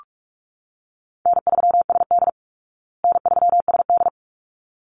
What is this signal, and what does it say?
Morse code, heard as a keyed tone, about 44 wpm, 700 Hz. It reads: N3SD N3SD
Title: Easy Call Signs Repeated 2x - 45wpm